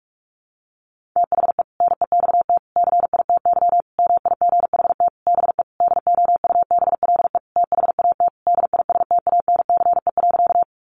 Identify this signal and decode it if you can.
Morse code, heard as a keyed tone, about 44 wpm, 700 Hz. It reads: THE NEXT CITY MIGHT BE DOUBLE THAT DISTANCE.
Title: Sentences from Top 1400 Words 45wpm